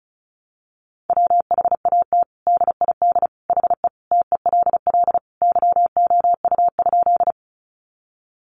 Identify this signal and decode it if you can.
Morse code, heard as a keyed tone, about 34 wpm, 700 Hz. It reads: WHAT DID HE TELL YOU?
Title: Sentences from Top 200 Words 35wpm